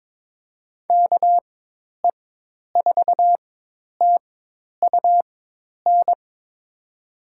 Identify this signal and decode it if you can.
Morse code, heard as a keyed tone, about 22 wpm, spaced as for 10 wpm, 700 Hz. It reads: KE4TUN
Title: USA Call Signs - 6 Characters - 4x Character Spacing - 22wpm